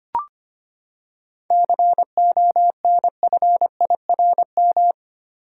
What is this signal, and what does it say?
Morse code, heard as a keyed tone, about 25 wpm, 700 Hz. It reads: CONFIRM
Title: Seven Letter Words - 25wpm